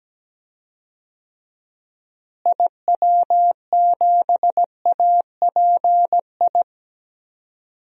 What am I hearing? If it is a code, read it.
Morse code, heard as a keyed tone, about 17 wpm, 700 Hz. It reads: IW7API
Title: DXCC-100 Call Signs - 6 Characters - Repeated 3x - 17wpm